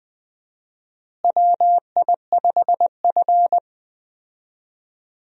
Morse code, heard as a keyed tone, about 20 wpm, 700 Hz. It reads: WI5F
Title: USA Call Signs - 4 Characters - Repeated 3x - 20wpm